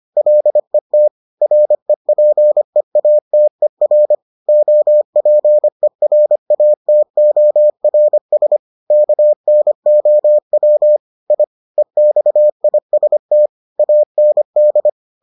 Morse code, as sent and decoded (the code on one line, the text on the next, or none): .-.. - / .-. .--. .- - . .-. / --- .--. .-. .- - --- .-. ... / -.- -. --- .-- / .. / . -..- .. ... - / .- -. -..
LET REPEATER OPERATORS KNOW I EXIST AND